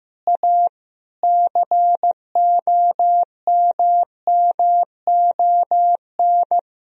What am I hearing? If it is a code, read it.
Morse code, heard as a keyed tone, about 15 wpm, 700 Hz. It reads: A COMMON
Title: Sentences from Top 1000 Words - Review Part 3 - 15wpm